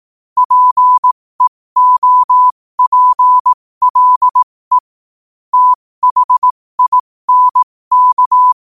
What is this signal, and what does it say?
Morse code, heard as a keyed tone, about 18 wpm, spaced as for 15 wpm, 1000 Hz. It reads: PEOPLE THINK
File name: qotdfiles / 15wpm